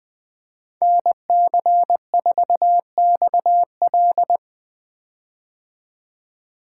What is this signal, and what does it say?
Morse code, heard as a keyed tone, about 20 wpm, 700 Hz. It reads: NC4XL
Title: 2023 POTA Activator Call Signs - 20wpm